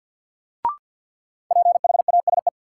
Morse code, heard as a keyed tone, about 52 wpm, 700 Hz. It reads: PHASE
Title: Sentences from Top 1200 Words - New Word Drill - 50wpm